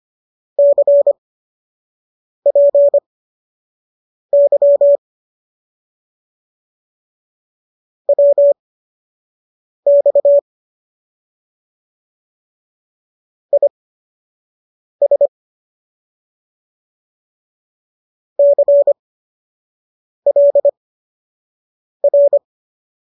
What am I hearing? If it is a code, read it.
Morse code, heard as a keyed tone, about 25 wpm, spaced as for 6 wpm, 570 Hz. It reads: CPY WX IS CLR
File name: qso003 06